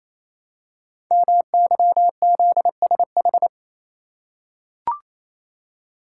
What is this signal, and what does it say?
Morse code, heard as a keyed tone, about 27 wpm, 700 Hz. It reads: MYZSH
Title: Code Groups - 5 Letter:Numbers - 28wpm